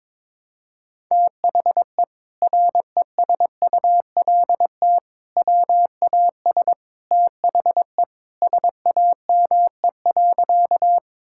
Morse code, heard as a keyed tone, about 22 wpm, 700 Hz. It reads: THE RESULT WAS THE SAME.
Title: Sentences from Top 1000 Words - Review Part 5 - 22wpm